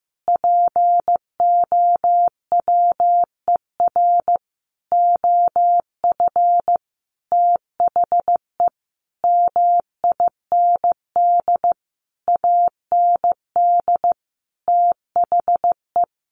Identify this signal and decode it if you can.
Morse code, heard as a keyed tone, about 15 wpm, 700 Hz. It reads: POWER OF THE MIND AND THE
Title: Sentences from Top 1400 Words 15wpm